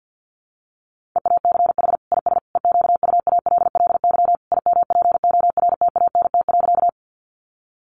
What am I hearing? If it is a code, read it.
Morse code, heard as a keyed tone, about 50 wpm, 700 Hz. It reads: EACH IS EQUALLY IMPORTANT.
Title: Sentences from Top 1400 Words 50wpm